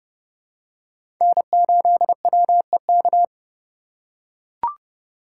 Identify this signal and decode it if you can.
Morse code, heard as a keyed tone, about 30 wpm, 700 Hz. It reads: N8WEK
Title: USA Call Signs - 5 Characters - 3x Character Spacing - 30wpm